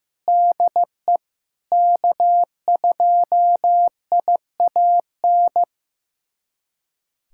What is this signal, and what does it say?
Morse code, heard as a keyed tone, about 15 wpm, 700 Hz. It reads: DE K2IAN
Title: USA Callsign DE Callsign - 5 Characters - 15wpm